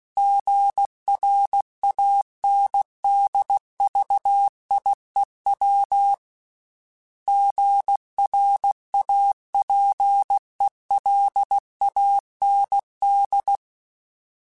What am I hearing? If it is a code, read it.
Morse code, heard as a keyed tone, about 16 wpm, 800 Hz. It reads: GRANDVIEW GRAPELAND
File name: fromSMs / FromSMs mp3s / Cities@15